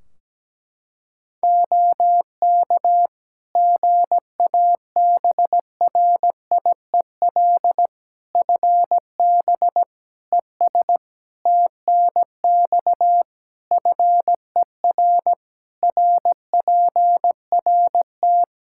Morse code, heard as a keyed tone, about 17 wpm, 700 Hz. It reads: OK GABRIEL FB ES TNX FER RPRT
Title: Practice QSO - Exchange 2 - Info - 17wpm